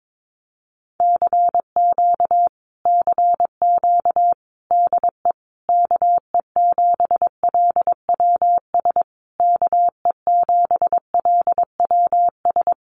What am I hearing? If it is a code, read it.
Morse code, heard as a keyed tone, about 22 wpm, 700 Hz. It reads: CQ CQ DE KE7LWS KE7LWS
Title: Common QSO Phrases Encore 22wpm